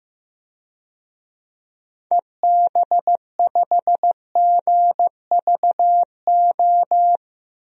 Morse code, heard as a keyed tone, about 15 wpm, 700 Hz. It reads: EB5GVO
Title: DXCC-100 Call Signs - 6 Characters - Repeated 3x - 15wpm